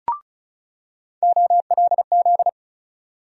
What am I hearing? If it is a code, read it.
Morse code, heard as a keyed tone, about 36 wpm, 700 Hz. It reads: OLZ
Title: USA Call Sign Postfixes - 3 Letters 35wpm